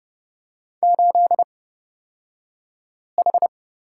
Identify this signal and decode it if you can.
Morse code, heard as a keyed tone, about 30 wpm, spaced as for 5 wpm, 700 Hz. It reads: 8H
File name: Lesson 37d - Letter x - Two Characters 30wpm